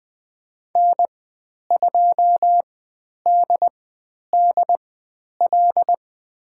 Morse code, heard as a keyed tone, about 20 wpm, spaced as for 10 wpm, 700 Hz. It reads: N2DDL
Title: 109 Lesson 18g - Letter m - Call Signs 20wpm